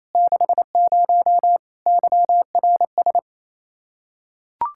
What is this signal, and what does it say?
Morse code, heard as a keyed tone, about 28 wpm, 700 Hz. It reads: 60 YRS
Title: Practice QSO - Exchange 3 - More Info - 28wpm